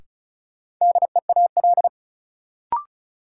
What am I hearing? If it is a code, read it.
Morse code, heard as a keyed tone, about 36 wpm, 700 Hz. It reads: DEAL
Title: Sets of 3 Words - New Word Drill - 35wpm